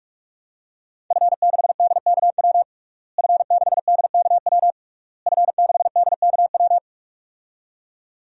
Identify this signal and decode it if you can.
Morse code, heard as a keyed tone, about 46 wpm, 700 Hz. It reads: F6DKW F6DKW F6DKW
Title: DXCC-100 Call Signs - 5 Characters - Repeated 3x - 45wpm